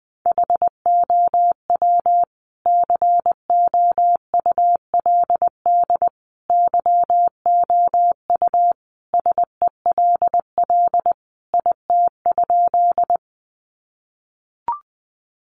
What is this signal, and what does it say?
Morse code, heard as a keyed tone, about 20 wpm, 700 Hz. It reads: HOW COULD YOU SELL IT?
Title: Sentences from Top 1100 Words 20wpm